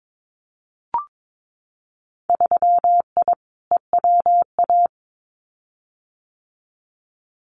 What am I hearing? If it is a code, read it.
Morse code, heard as a keyed tone, about 22 wpm, 700 Hz. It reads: 3I EWA
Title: Winter Field Day Exchanges Encore 22wpm